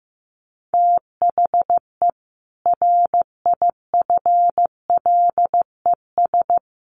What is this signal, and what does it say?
Morse code, heard as a keyed tone, about 15 wpm, 700 Hz. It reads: THE RIFLES
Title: A Sound Of Thunder 15wpm